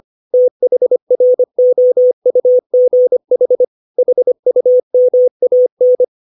Morse code, heard as a keyed tone, about 25 wpm, 500 Hz. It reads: THROUGH HUMAN
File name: 632 words / 632 Words 25WPM